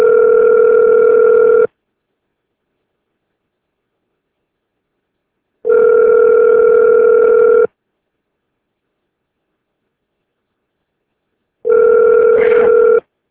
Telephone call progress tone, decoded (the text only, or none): ringback tone